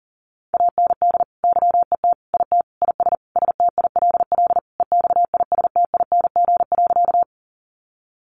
Morse code, heard as a keyed tone, about 40 wpm, 700 Hz. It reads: AND YET IT IS STILL EXISTING.
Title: Sentences from Top 1500 Words 40wpm